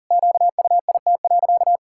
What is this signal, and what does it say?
Morse code, heard as a keyed tone, about 40 wpm, 700 Hz. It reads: QUIT.
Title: Pangram Challenge v2 - 40wpm